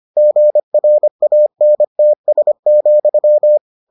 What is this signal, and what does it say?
Morse code, heard as a keyed tone, about 25 wpm, 600 Hz. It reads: GRANTS,